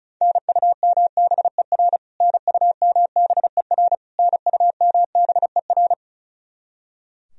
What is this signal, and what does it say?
Morse code, heard as a keyed tone, about 36 wpm, 700 Hz. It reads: NUMBER NUMBER NUMBER
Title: Top 100 Words Repeated 35wpm